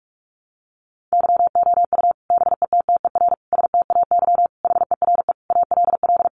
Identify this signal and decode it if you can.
Morse code, heard as a keyed tone, about 44 wpm, 700 Hz. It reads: YOU BETTER STAY HERE ALL
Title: Sentences from Top 500 Words 45wpm